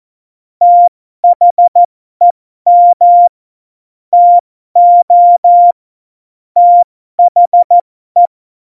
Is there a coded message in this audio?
Yes: Morse code from 0.6 to 8.3 s, heard as a keyed tone, about 14 wpm, spaced as for 12 wpm, 700 Hz. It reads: THEM TO THE